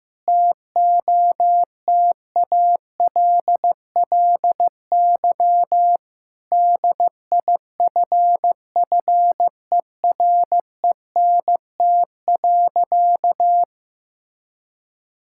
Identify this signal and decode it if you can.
Morse code, heard as a keyed tone, about 15 wpm, 700 Hz. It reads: TOTALLY DIFFERENT.
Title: Sentences from Top 1600 Words 15wpm